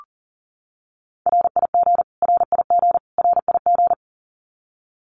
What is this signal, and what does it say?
Morse code, heard as a keyed tone, about 40 wpm, 700 Hz. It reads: RIG RIG RIG
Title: Abbreviations Repeated 3x - 40wpm